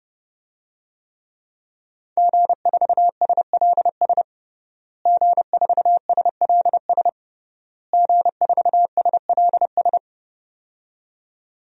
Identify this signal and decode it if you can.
Morse code, heard as a keyed tone, about 30 wpm, 700 Hz. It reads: G4SLS G4SLS G4SLS
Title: DXCC-100 Call Signs - 5 Characters - Repeated 3x - 30wpm